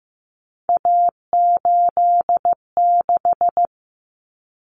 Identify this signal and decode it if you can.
Morse code, heard as a keyed tone, about 15 wpm, 700 Hz. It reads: A86
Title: Common Cut Numbers - 3-Digit 15wpm